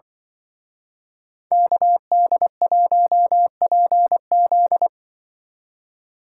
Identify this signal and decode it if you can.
Morse code, heard as a keyed tone, about 24 wpm, 700 Hz. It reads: KD1PZ
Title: Speed Racing - USA 2x2 Call Signs 20wpm